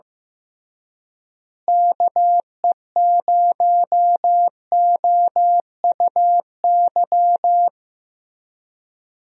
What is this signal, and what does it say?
Morse code, heard as a keyed tone, about 16 wpm, 700 Hz. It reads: KE0OUY
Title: USA Call Signs - 6 Characters - Repeated 3x - 15wpm